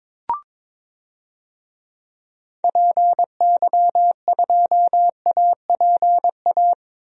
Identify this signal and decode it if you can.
Morse code, heard as a keyed tone, about 22 wpm, 700 Hz. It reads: PY2APA
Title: DXCC-100 Call Signs - 6 Characters - 22wpm